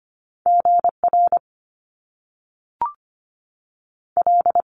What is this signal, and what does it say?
Morse code, heard as a keyed tone, about 25 wpm, 700 Hz. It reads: GR L
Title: Top Two Letter Combinations in Words 25wpm